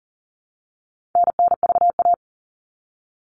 Morse code, heard as a keyed tone, about 40 wpm, 700 Hz. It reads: NNVA
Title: All Cut Numbers - 4-Digit 40wpm